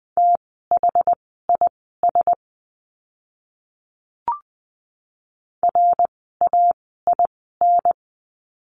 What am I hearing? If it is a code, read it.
Morse code, heard as a keyed tone, about 20 wpm, spaced as for 14 wpm, 700 Hz. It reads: THIS RAIN
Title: 061 Lesson 11f - Letter h - Words 20wpm